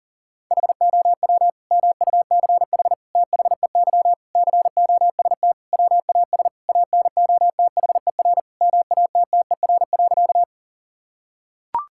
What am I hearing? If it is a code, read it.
Morse code, heard as a keyed tone, about 40 wpm, 700 Hz. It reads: HOW MUCH THEY COST WAS ANOTHER MATTER.
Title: Sentences from Top 1000 Words - Review Part 5 - 40wpm